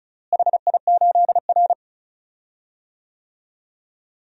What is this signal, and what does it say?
Morse code, heard as a keyed tone, about 36 wpm, 700 Hz. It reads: HI8R
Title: DXCC-100 Call Signs - 4 Characters - Repeated 2x - 35wpm